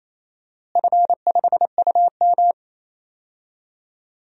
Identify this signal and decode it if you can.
Morse code, heard as a keyed tone, about 28 wpm, 700 Hz. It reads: F5UM
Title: DXCC-100 Call Signs - 4 Characters - Repeated 3x - 28wpm